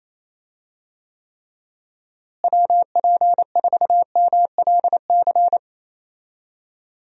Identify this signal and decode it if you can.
Morse code, heard as a keyed tone, about 28 wpm, 700 Hz. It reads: WP4MLC
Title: USA Call Signs - 6 Characters - Repeated 2x - 28wpm